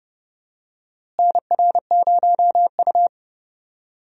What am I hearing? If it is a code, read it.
Morse code, heard as a keyed tone, about 31 wpm, 700 Hz. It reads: NR0U
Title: USA Call Signs - 4 Characters - Repeated 3x - 30wpm